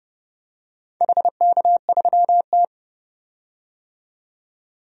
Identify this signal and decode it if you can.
Morse code, heard as a keyed tone, about 30 wpm, 700 Hz. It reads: HK3T
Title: DXCC-100 Call Signs - 4 Characters - 30wpm